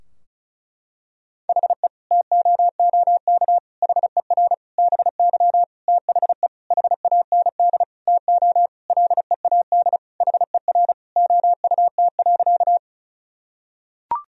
Morse code, heard as a keyed tone, about 34 wpm, 700 Hz. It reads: HE TOOK HER BY THE HAND TO LEAD HER OUT.
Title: Sentences from Top 1000 Words - Review Part 2 - 35wpm